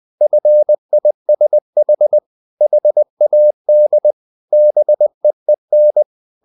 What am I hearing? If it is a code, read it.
Morse code, heard as a keyed tone, about 20 wpm, 600 Hz. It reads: FISH HAD BEEN
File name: the old man and the sea part 2 20wpm